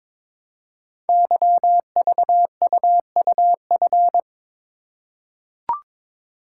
Morse code, heard as a keyed tone, about 22 wpm, 700 Hz. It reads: YVUUF
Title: Code Groups - 5 Letter:Numbers - 22wpm